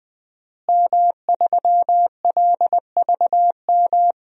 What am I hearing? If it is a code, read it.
Morse code, heard as a keyed tone, about 20 wpm, 700 Hz. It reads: M3LVM
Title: DXCC-100 Call Signs - 5 Characters - Repeated 2x - 20wpm